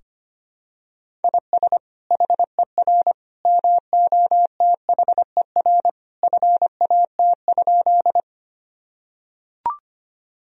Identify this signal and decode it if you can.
Morse code, heard as a keyed tone, about 25 wpm, 700 Hz. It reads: IS HER MOTHER FAT?
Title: Sentences from Top 1000 Words - Review Part 4 - 25wpm